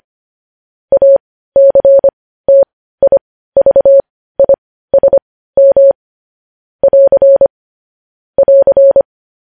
Morse code, heard as a keyed tone, about 25 wpm, spaced as for 15 wpm, 550 Hz. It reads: ACTIVISM <AR> <AR>
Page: news-headlines-in-morse-code-at-15-wpm